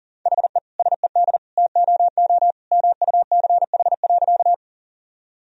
Morse code, heard as a keyed tone, about 40 wpm, 700 Hz. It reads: HE SED TOO MUCH.
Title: Sentences from Top 200 Words and Abbreviations 40wpm